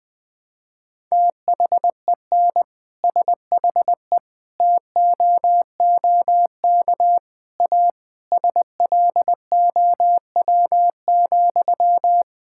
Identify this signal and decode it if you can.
Morse code, heard as a keyed tone, about 20 wpm, 700 Hz. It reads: THEN SHE TOOK A SLOW,
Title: Sentences from Top 1600 Words 20wpm